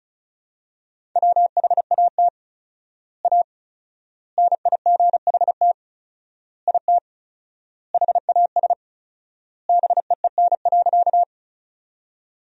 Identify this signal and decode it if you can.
Morse code, heard as a keyed tone, about 34 wpm, 700 Hz. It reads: WHAT A NIGHT IT HAS BEEN.
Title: 4x Word-Spacing - Sentences from Top 300 Words - 35wpm